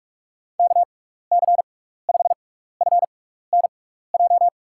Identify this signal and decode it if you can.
Morse code, heard as a keyed tone, about 46 wpm, spaced as for 15 wpm, 700 Hz. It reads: KC5FNJ
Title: USA Call Signs - 6 Characters - 6x Character Spacing - 45wpm